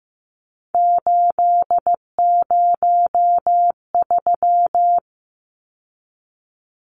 Morse code, heard as a keyed tone, about 15 wpm, 700 Hz. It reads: 803
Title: Numbers - 3-Digit 15wpm